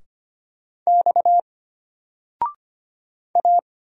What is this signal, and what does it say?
Morse code, heard as a keyed tone, about 25 wpm, 700 Hz. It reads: X A
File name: Single-Letter-Number-25wpm